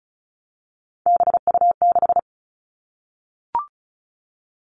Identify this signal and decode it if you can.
Morse code, heard as a keyed tone, about 36 wpm, 700 Hz. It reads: BU6
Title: Three Challenging Letter-Number Combinations 35wpm